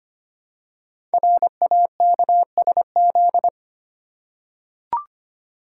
Morse code, heard as a keyed tone, about 25 wpm, 700 Hz. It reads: RAKSZ
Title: Code Groups - 5 Letter:Numbers - 25wpm